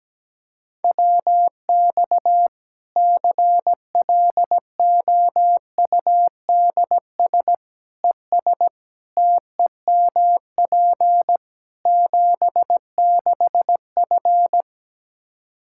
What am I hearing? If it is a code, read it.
Morse code, heard as a keyed tone, about 17 wpm, 700 Hz. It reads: WX CLOUDS ES TEMP 76F
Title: K1USN SST QSO Practice 17wpm